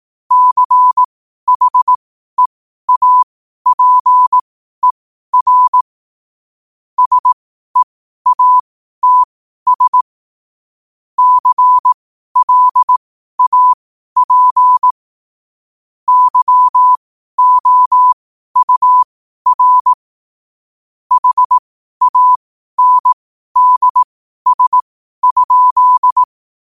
Morse code, heard as a keyed tone, about 18 wpm, spaced as for 12 wpm, 1000 Hz. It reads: CHEAPER SEATS CLAP YOUR HANDS?
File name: qotdfiles / 12wpm